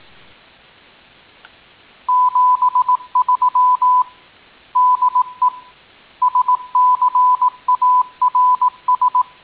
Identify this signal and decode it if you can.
Morse code, heard as a keyed tone, about 18 wpm, 1000 Hz. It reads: 73 DE SCARS
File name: tin-upgrade / QSO 2019 08 04